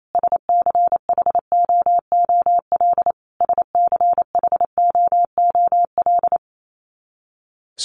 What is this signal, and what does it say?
Morse code, heard as a keyed tone, about 28 wpm, 700 Hz. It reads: SCHOOL SCHOOL
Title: Top 500 Words Repeated 28wpm